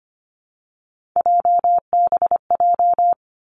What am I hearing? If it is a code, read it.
Morse code, heard as a keyed tone, about 25 wpm, 700 Hz. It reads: JBJ
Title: Three Challenging Letter-Number Combinations 25wpm